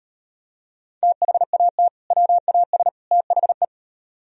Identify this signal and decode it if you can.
Morse code, heard as a keyed tone, about 38 wpm, 700 Hz. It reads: THAT WAS THE